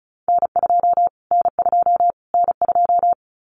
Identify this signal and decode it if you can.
Morse code, heard as a keyed tone, about 34 wpm, 700 Hz. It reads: N2 N2 N2
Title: Callsign Prefixes for Top 25 DXCC - Repeated 3x - 35wpm